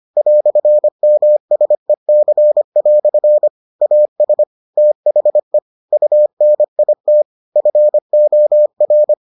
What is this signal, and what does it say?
Morse code, heard as a keyed tone, about 25 wpm, 600 Hz. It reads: "MSEC" AS THE UNIT FOR